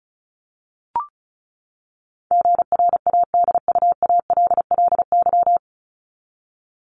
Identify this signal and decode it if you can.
Morse code, heard as a keyed tone, about 34 wpm, 700 Hz. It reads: GRADUALLY
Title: LY Suffix - Top 100 Words with Base Words 35wpm